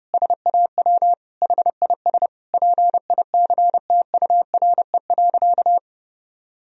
Morse code, heard as a keyed tone, about 31 wpm, 700 Hz. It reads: SAW HIS PICTURE.
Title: Sentences from Top 1000 Words - Review Part 1 - 30wpm